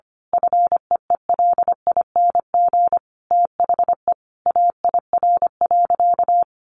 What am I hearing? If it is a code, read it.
Morse code, heard as a keyed tone, about 25 wpm, 700 Hz. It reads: FEELING THE AIR.
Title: A Sound Of Thunder 25wpm